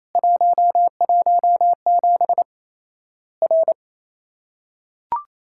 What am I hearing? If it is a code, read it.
Morse code, heard as a keyed tone, about 27 wpm, 700 Hz. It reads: 117 R
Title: Worked All Europe DX Contest - QTC Exchange - 28wpm